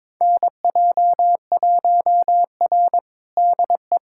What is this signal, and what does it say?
Morse code, heard as a keyed tone, about 22 wpm, 700 Hz. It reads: NJ1R DE
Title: USA Callsign DE Callsign - 4 Characters - 22wpm